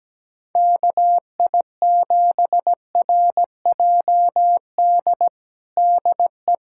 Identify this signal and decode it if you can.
Morse code, heard as a keyed tone, about 17 wpm, 700 Hz. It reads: KI7RJD DE
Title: USA Callsign DE Callsign - 6 Characters - 17wpm